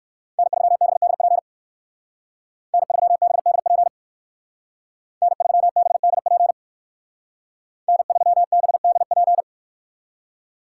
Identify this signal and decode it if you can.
Morse code, heard as a keyed tone, about 58 wpm, 700 Hz. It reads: N3BDP N3BDP N3BDP N3BDP